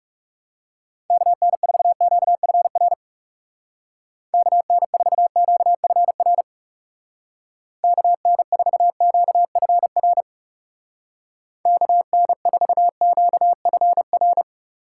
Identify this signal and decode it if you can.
Morse code, heard as a keyed tone, about 36 wpm, 700 Hz. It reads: KN4QFR KN4QFR KN4QFR KN4QFR
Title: Speed Racing - USA Call Signs 30wpm